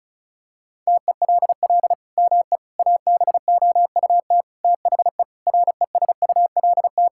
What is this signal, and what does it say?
Morse code, heard as a keyed tone, about 36 wpm, 700 Hz. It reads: TELL ME ABOUT THE RESULT
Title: Sentences from Top 1000 Words - Review Part 5 - 35wpm